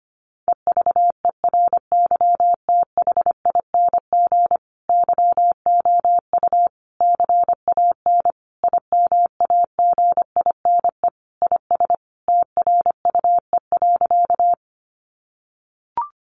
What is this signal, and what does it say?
Morse code, heard as a keyed tone, about 25 wpm, 700 Hz. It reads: EVERYTHING YOU CAN IMAGINE IS TRUE.
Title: Sentences from Top 1400 Words 25wpm